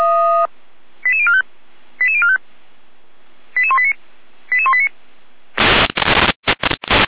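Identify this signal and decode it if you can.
Five-tone selective call (ZVEI), heard as quick runs of five tones, tones of about 70 ms. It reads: 88045 88045 88189 88189
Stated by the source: ZVEI five-tone selective call 88189, received on an SDR.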